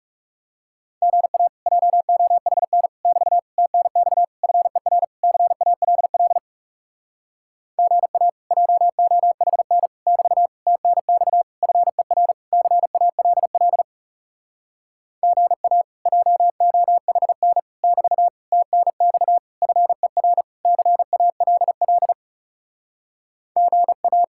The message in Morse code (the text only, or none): GA JOHN <BT> TNX FER CALL GA JOHN <BT> TNX FER CALL GA JOHN <BT> TNX FER CALL GA